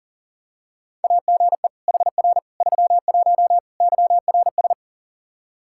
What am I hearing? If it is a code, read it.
Morse code, heard as a keyed tone, about 37 wpm, 700 Hz. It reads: AGE HR 31 YRS
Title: Practice QSO - Exchange 3 - More Info - 40wpm